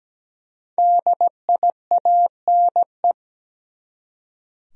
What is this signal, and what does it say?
Morse code, heard as a keyed tone, about 17 wpm, 700 Hz. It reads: DIANE
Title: USA Common Names - Max 5 Letters - 17wpm